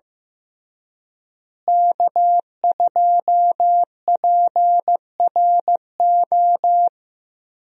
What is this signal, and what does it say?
Morse code, heard as a keyed tone, about 15 wpm, 700 Hz. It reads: K2PRO
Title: USA Call Signs - 5 Characters - Repeated 3x - 15wpm